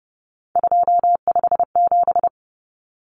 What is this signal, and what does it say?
Morse code, heard as a keyed tone, about 30 wpm, 700 Hz. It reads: 257
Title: Common Cut Numbers - 3-Digit 30wpm